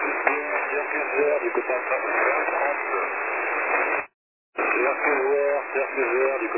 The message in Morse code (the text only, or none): none